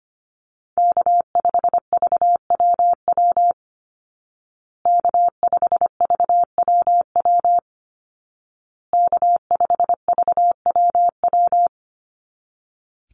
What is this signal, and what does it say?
Morse code, heard as a keyed tone, about 25 wpm, 700 Hz. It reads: K5VWW K5VWW K5VWW
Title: Call Signs Repeated 2x - 25wpm